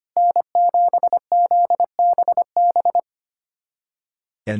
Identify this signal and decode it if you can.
Morse code, heard as a keyed tone, about 25 wpm, 700 Hz. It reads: N7ZBB
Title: USA Callsign DE Callsign - 5 Characters - 25wpm